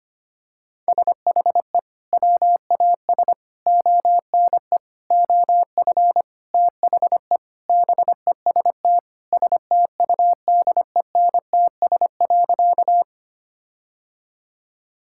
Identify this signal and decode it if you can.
Morse code, heard as a keyed tone, about 25 wpm, 700 Hz. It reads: SHE WAS ONE OF THE BEST STUDENTS.
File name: practice / 25wpm / Sentences with 4-Word Phrases - 25wpm